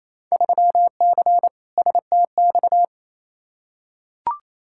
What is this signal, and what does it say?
Morse code, heard as a keyed tone, about 28 wpm, 700 Hz. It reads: 3C STX
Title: Field Day Exchanges Encore - 28wpm